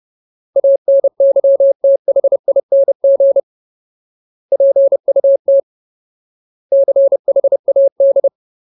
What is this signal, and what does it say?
Morse code, heard as a keyed tone, about 30 wpm, 550 Hz. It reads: ANYTHING PUT CHAD
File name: Ultimate Word List / 30WPM Ultimate